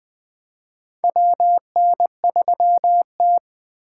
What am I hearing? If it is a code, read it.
Morse code, heard as a keyed tone, about 20 wpm, 700 Hz. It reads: WN3T
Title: USA Call Signs - 4 Characters - 4x Character Spacing - 20wpm